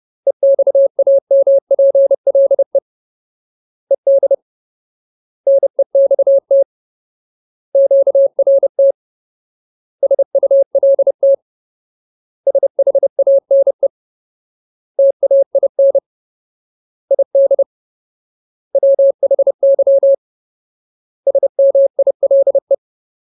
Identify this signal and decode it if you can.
Morse code, heard as a keyed tone, about 30 wpm, 550 Hz. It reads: EXAMPLE ED NEXT QRT SULT SHANE TAIN ID WHY SMILE
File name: Ultimate Word List / 30WPM Ultimate